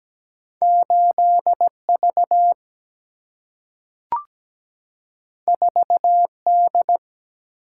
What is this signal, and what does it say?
Morse code, heard as a keyed tone, about 17 wpm, 700 Hz. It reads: 8V 4D
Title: Single-Letter Single-Number 17wpm